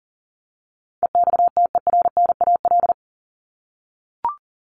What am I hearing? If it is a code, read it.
Morse code, heard as a keyed tone, about 40 wpm, 700 Hz. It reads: EXTERNAL
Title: Top 1751-2000 Words 40wpm